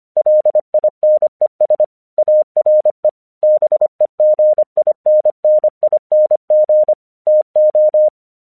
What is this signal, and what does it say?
Morse code, heard as a keyed tone, about 25 wpm, 610 Hz. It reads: LINES ARE BEGINNING TO
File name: 1 English 25wpm / AQC16